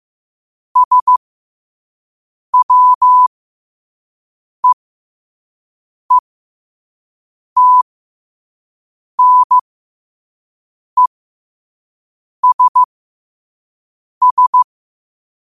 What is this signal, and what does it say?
Morse code, heard as a keyed tone, about 15 wpm, spaced as for 5 wpm, 1000 Hz. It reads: SWEETNESS
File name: qotdfiles / 5wpm